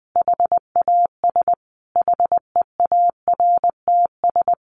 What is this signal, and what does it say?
Morse code, heard as a keyed tone, about 20 wpm, 700 Hz. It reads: HAS HEARTS
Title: Additional Easy Phrases 20wpm